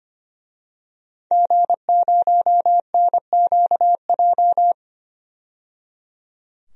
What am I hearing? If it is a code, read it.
Morse code, heard as a keyed tone, about 25 wpm, 700 Hz. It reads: G0NQJ
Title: DXCC-100 Call Signs - 5 Characters - 25wpm